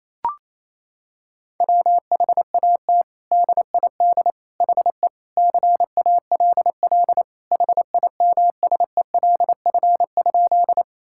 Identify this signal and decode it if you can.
Morse code, heard as a keyed tone, about 28 wpm, 700 Hz. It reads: WHAT DID HE CALL HIMSELF?
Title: Sentences from Top 1600 Words 28wpm